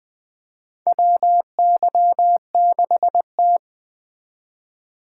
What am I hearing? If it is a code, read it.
Morse code, heard as a keyed tone, about 20 wpm, 700 Hz. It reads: WY6T
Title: USA Call Signs - 4 Characters - 4x Character Spacing - 20wpm